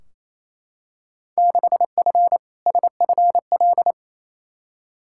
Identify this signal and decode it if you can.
Morse code, heard as a keyed tone, about 28 wpm, 700 Hz. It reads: 6F SFL
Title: Field Day Exchanges Encore - 28wpm